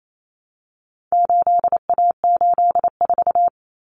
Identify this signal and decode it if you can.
Morse code, heard as a keyed tone, about 28 wpm, 700 Hz. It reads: 8A84
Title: Common Cut Numbers - 4-Digit 28wpm